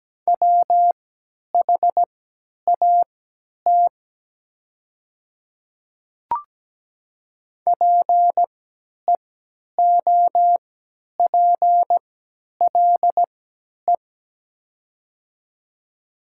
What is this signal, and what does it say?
Morse code, heard as a keyed tone, about 17 wpm, spaced as for 10 wpm, 700 Hz. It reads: WHAT PEOPLE
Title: 3x Character Spacing - Top 100 Words - 17wpm